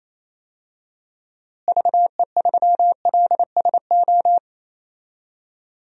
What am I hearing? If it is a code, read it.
Morse code, heard as a keyed tone, about 27 wpm, 700 Hz. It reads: VE3LSO
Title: DXCC-100 Call Signs - 6 Characters - Repeated 2x - 28wpm